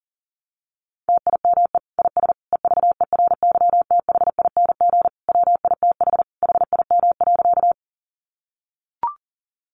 Morse code, heard as a keyed tone, about 40 wpm, 700 Hz. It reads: TIME IS EVERYTHING WITH HIM.
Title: Sentences from Top 700 Words 40wpm